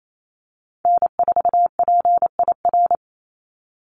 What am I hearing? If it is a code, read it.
Morse code, heard as a keyed tone, about 28 wpm, 700 Hz. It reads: N4PIR
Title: Call Signs Repeated 2x - 28wpm